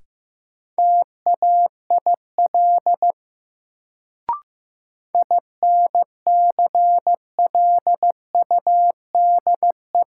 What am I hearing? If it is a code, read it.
Morse code, heard as a keyed tone, about 15 wpm, 700 Hz. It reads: TAIL INCLUDE
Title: Sentences from Top 1000 Words - Review Part 3 - New Word Drill - 15wpm